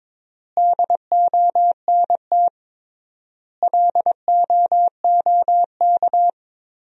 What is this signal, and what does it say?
Morse code, heard as a keyed tone, about 22 wpm, 700 Hz. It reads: DONT LOOK
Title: 3x Word-Spacing - Sentences from Top 300 Words - 22wpm